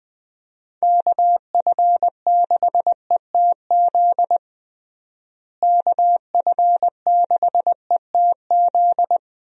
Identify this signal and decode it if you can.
Morse code, heard as a keyed tone, about 21 wpm, 700 Hz. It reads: KF6ETZ KF6ETZ
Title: USA Call Signs - 6 Characters - Repeated 3x - 20wpm